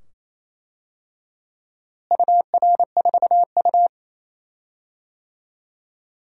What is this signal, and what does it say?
Morse code, heard as a keyed tone, about 28 wpm, 700 Hz. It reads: UR4U
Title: DXCC-100 Call Signs - 4 Characters - Repeated 2x - 28wpm